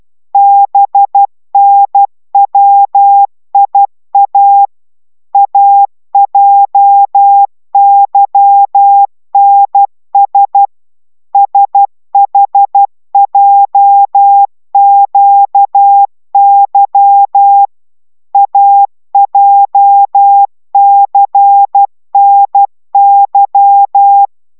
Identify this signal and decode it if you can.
Morse code, heard as a keyed tone, about 12 wpm, 800 Hz. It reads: BNWIA AJYNS SHJQY AJCNY